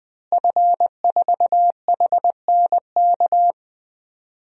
Morse code, heard as a keyed tone, about 21 wpm, 700 Hz. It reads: F4HNK
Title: DXCC-100 Call Signs - 5 Characters - Repeated 2x - 20wpm